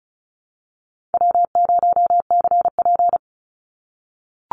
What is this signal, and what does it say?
Morse code, heard as a keyed tone, about 34 wpm, 700 Hz. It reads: W0CP
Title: NA SOTA Activator Callsigns 35wpm